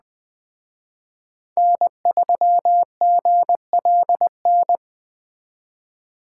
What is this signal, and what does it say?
Morse code, heard as a keyed tone, about 20 wpm, 700 Hz. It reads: N3GLN